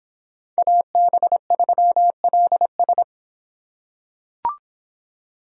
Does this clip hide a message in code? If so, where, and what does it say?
Yes: Morse code from 0.6 to 3.0 s, heard as a keyed tone, about 26 wpm, 700 Hz. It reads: AB3LS